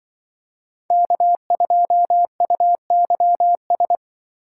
Morse code, heard as a keyed tone, about 24 wpm, 700 Hz. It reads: K2UYS
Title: Speed Racing - USA 1x3 Call Signs 20wpm